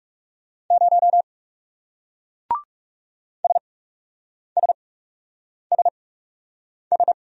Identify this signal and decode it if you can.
Morse code, heard as a keyed tone, about 38 wpm, spaced as for 8 wpm, 700 Hz. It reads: 0 SSSS